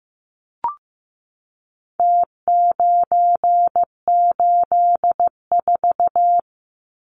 Morse code, heard as a keyed tone, about 15 wpm, 700 Hz. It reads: T984
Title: Numbers - Contest Style - 4-Digit 15wpm